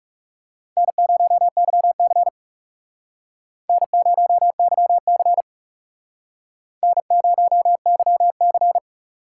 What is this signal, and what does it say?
Morse code, heard as a keyed tone, about 39 wpm, 700 Hz. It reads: N0YC N0YC N0YC